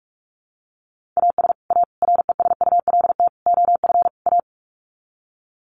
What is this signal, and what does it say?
Morse code, heard as a keyed tone, about 44 wpm, 700 Hz. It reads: AS A RESULT OF A
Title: Sets of 5 Words Encore 45wpm